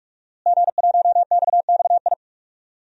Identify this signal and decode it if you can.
Morse code, heard as a keyed tone, about 43 wpm, 700 Hz. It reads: G1XXI
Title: DXCC-100 Call Signs - 5 Characters - Repeated 2x - 45wpm